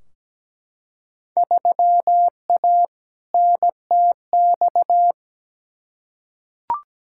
Morse code, heard as a keyed tone, about 17 wpm, 700 Hz. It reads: 3A NTX
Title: Field Day Exchanges - 17wpm